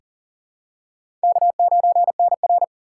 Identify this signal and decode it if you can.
Morse code, heard as a keyed tone, about 39 wpm, 700 Hz. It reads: K9NR
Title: K1USN SST QSO Practice - 40wpm